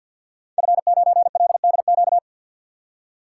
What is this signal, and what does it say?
Morse code, heard as a keyed tone, about 52 wpm, 700 Hz. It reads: F9LDQ
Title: Code Groups - 5 Letter:Numbers - 50wpm